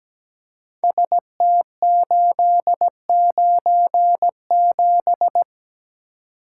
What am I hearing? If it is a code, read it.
Morse code, heard as a keyed tone, about 17 wpm, 700 Hz. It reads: ST897
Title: Code Groups - 5 Letter:Numbers - 17wpm